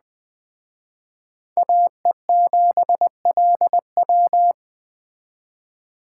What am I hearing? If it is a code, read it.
Morse code, heard as a keyed tone, about 20 wpm, 700 Hz. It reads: AE7LW